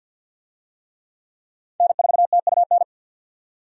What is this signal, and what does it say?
Morse code, heard as a keyed tone, about 52 wpm, 700 Hz. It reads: N4TUN
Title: USA Call Signs - 5 Characters - 50wpm